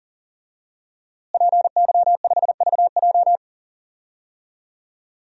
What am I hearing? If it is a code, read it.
Morse code, heard as a keyed tone, about 40 wpm, 700 Hz. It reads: PY5VJ
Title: DXCC-100 Call Signs - 5 Characters - Repeated 3x - 40wpm